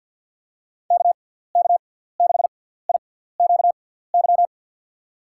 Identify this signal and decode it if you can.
Morse code, heard as a keyed tone, about 51 wpm, spaced as for 17 wpm, 700 Hz. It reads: KK6IQY
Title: USA Call Signs - 6x Character Spacing - 50wpm